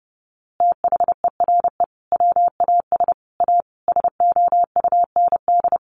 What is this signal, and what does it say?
Morse code, heard as a keyed tone, about 30 wpm, 700 Hz. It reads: THERE WAS A SOUND
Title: A Sound Of Thunder 30wpm